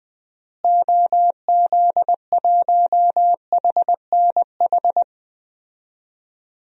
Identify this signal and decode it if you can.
Morse code, heard as a keyed tone, about 20 wpm, 700 Hz. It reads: OZ1HNH
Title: DXCC-100 Call Signs - 6 Characters - Repeated 3x - 20wpm